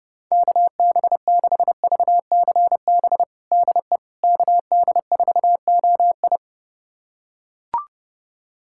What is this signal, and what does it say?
Morse code, heard as a keyed tone, about 30 wpm, 700 Hz. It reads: KB6VCB DE KD4OI
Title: USA Callsign DE Callsign 30wpm